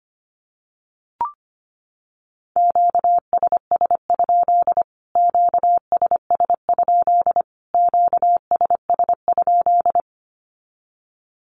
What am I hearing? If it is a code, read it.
Morse code, heard as a keyed tone, about 25 wpm, 700 Hz. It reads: QSS? QSS? QSS?
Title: Q-Codes Repeated 3x 25wpm